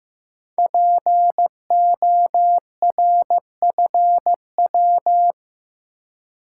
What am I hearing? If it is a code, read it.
Morse code, heard as a keyed tone, about 15 wpm, 700 Hz. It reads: PORFW
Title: Code Groups - 5 Letters - 15wpm